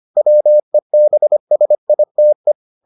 Morse code, heard as a keyed tone, about 25 wpm, 600 Hz. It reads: WEBSITE